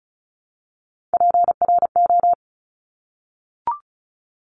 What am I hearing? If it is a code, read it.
Morse code, heard as a keyed tone, about 36 wpm, 700 Hz. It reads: PRO
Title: Top Three Letter Combinations in Words 35wpm